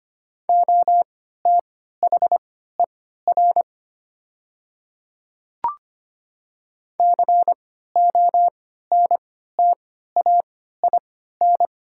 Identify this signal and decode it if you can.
Morse code, heard as a keyed tone, about 25 wpm, spaced as for 14 wpm, 700 Hz. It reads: OTHER CONTAIN
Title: 3x Character Spacing - Top 500 Words - 25wpm